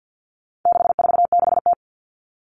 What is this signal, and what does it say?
Morse code, heard as a keyed tone, about 48 wpm, 700 Hz. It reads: 646T
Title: All Cut Numbers - 4-Digit 50wpm